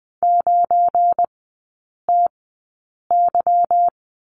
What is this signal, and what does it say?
Morse code, heard as a keyed tone, about 20 wpm, spaced as for 8 wpm, 690 Hz. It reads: 9TY
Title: 198 Lesson 32e - Letter k - Three Characters 20wpm